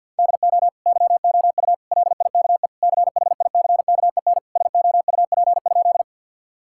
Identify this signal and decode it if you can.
Morse code, heard as a keyed tone, about 50 wpm, 700 Hz. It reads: DO YOU LIKE CHICKEN SOUP?
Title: Sentences from Top 1000 Words - Review Part 6 - 50wpm